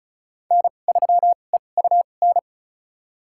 Morse code, heard as a keyed tone, about 34 wpm, spaced as for 25 wpm, 690 Hz. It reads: N3EUN
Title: USA Call Signs - 5 Characters - 2x Character Spacing - 35wpm